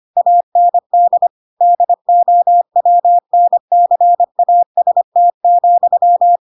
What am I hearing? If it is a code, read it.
Morse code, heard as a keyed tone, about 25 wpm, 700 Hz. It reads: AND DOWNCAST,
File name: Pinocchio / Pinocchio ch3 25 25 ik1hge